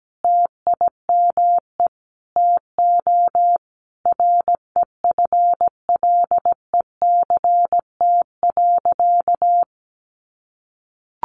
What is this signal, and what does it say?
Morse code, heard as a keyed tone, about 17 wpm, 700 Hz. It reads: TIME TO REFLECT.
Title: Sentences from Top 1500 Words 17wpm